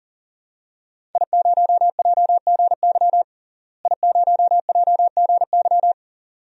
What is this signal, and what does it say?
Morse code, heard as a keyed tone, about 40 wpm, 700 Hz. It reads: I0JGY I0JGY
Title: DXCC-100 Call Signs - 5 Characters - Repeated 2x - 40wpm